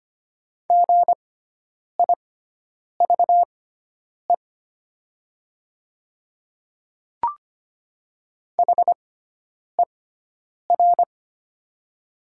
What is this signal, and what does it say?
Morse code, heard as a keyed tone, about 25 wpm, spaced as for 9 wpm, 690 Hz. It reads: GIVE HER
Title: 6x Character Spacing - Top 500 Words - 25wpm